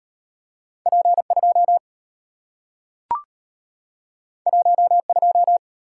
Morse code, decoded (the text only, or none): P2 12